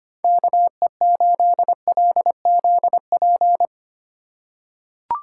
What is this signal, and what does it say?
Morse code, heard as a keyed tone, about 25 wpm, 700 Hz. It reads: KE8LZP
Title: USA Call Signs - 6 Characters - 3x Character Spacing - 25wpm